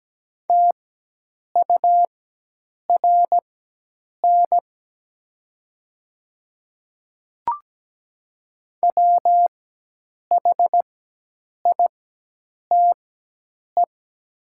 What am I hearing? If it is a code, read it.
Morse code, heard as a keyed tone, about 17 wpm, spaced as for 8 wpm, 700 Hz. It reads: TURN WHITE
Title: 4x Character Spacing - Top 500 Words - 17wpm